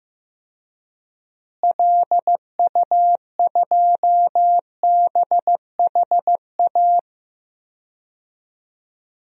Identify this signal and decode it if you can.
Morse code, heard as a keyed tone, about 15 wpm, 700 Hz. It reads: LU2BHA
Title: DXCC-100 Call Signs - 6 Characters - Repeated 2x - 15wpm